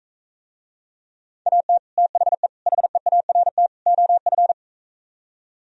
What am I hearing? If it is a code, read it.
Morse code, heard as a keyed tone, about 44 wpm, 700 Hz. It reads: AT THE HEART OF